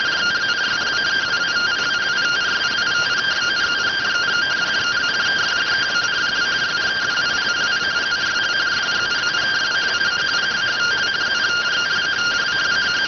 Two-tone teleprinter signal, about 170 Hz shift.